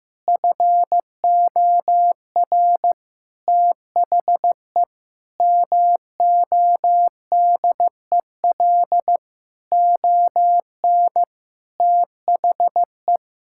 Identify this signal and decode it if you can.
Morse code, heard as a keyed tone, about 15 wpm, 700 Hz. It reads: FOR THE MODEL ON THE
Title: Sentences from Top 1000 Words - Review Part 5 - 15wpm